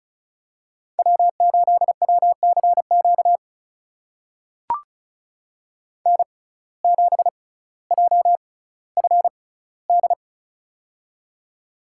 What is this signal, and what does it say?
Morse code, heard as a keyed tone, about 36 wpm, spaced as for 18 wpm, 700 Hz. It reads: W8WCQ N7JFD
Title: USA Call Signs - 5 Characters - 6x Character Spacing - 35wpm